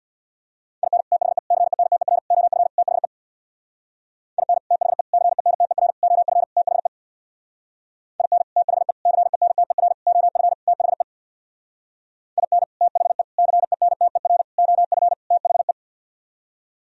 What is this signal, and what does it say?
Morse code, heard as a keyed tone, about 65 wpm, 700 Hz. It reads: IN THE CENTER OF THE IN THE CENTER OF THE IN THE CENTER OF THE IN THE CENTER OF THE